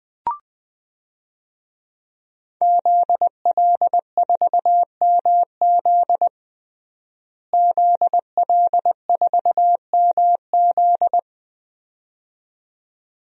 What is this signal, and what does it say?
Morse code, heard as a keyed tone, about 20 wpm, 700 Hz. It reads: ZL4MZ ZL4MZ
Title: DXCC-100 Call Signs - 5 Characters - Repeated 2x - 20wpm